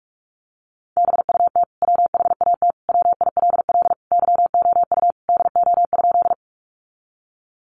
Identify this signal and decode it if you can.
Morse code, heard as a keyed tone, about 46 wpm, 700 Hz. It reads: BUT WHAT WILL YOU DO?
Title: Sentences from Top 100 Words 45wpm